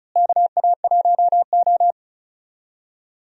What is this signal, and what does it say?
Morse code, heard as a keyed tone, about 34 wpm, 700 Hz. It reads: KA1O
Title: USA Call Signs - 4 Characters - Repeated 3x - 35wpm